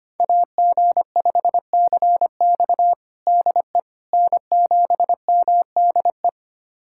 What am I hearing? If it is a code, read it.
Morse code, heard as a keyed tone, about 25 wpm, 700 Hz. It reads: AG5CX DE N7MDE
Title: USA Callsign DE Callsign - 5 Characters - 25wpm